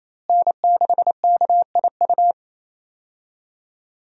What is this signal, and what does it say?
Morse code, heard as a keyed tone, about 28 wpm, 700 Hz. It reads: N6KIU
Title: USA Call Signs - 5 Characters - Repeated 3x - 28wpm